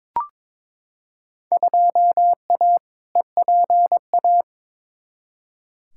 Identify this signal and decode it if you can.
Morse code, heard as a keyed tone, about 21 wpm, 700 Hz. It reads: 2A EPA
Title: Field Day Exchanges - 22wpm